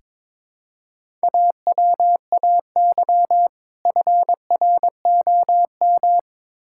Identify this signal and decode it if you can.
Morse code, heard as a keyed tone, about 22 wpm, 700 Hz. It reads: AWAY FROM
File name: Build A Phrase - 2-Word Phrases - 22wpm